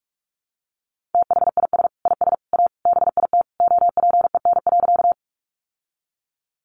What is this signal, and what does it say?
Morse code, heard as a keyed tone, about 44 wpm, 700 Hz. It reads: THIS IS A BIT OPEN.
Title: Sentences from Top 900 Words 45wpm